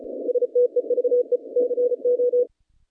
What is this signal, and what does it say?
Morse code, heard as a keyed tone, about 34 wpm, 490 Hz. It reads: STEVE CO
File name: contests / naqp-cw-k5tr-op-k5pi- -0300